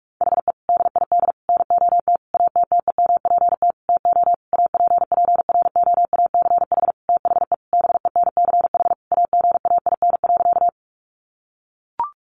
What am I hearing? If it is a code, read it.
Morse code, heard as a keyed tone, about 44 wpm, 700 Hz. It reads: HE DID NOT ATTEMPT TO APPROACH THE BENCH AGAIN.
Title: Sentences from Top 1400 Words 45wpm